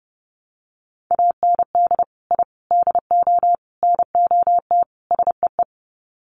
Morse code, heard as a keyed tone, about 30 wpm, 700 Hz. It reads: AND I DO NOT SEE
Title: Sets of 5 Words Encore2 30wpm